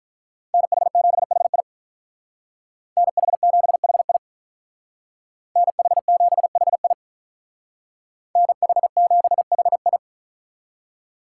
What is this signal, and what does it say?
Morse code, heard as a keyed tone, about 44 wpm, 700 Hz. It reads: NH7HI NH7HI NH7HI NH7HI